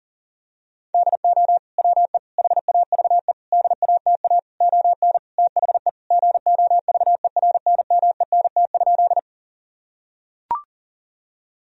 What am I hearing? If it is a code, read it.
Morse code, heard as a keyed tone, about 40 wpm, 700 Hz. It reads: DO WE HAVE DATA ON THE GOVERNMENT?
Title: Sentences from Top 1000 Words - Review Part 3 - 40wpm